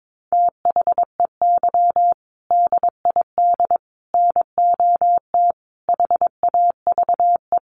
Morse code, heard as a keyed tone, about 22 wpm, 700 Hz. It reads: THEY DID NOT HAVE
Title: Sets of 5 Words Encore2 22wpm